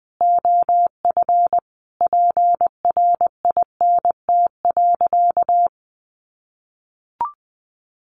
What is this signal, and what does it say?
Morse code, heard as a keyed tone, about 20 wpm, 700 Hz. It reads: OF PRINT.
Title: Sentences from Top 1400 Words 20wpm